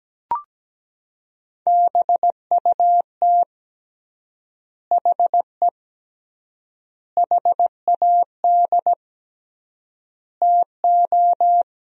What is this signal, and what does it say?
Morse code, heard as a keyed tone, about 17 wpm, 700 Hz. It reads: BUT HE HAD TO
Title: 3x Word-Spacing - Sentences from Top 100 Words - 17wpm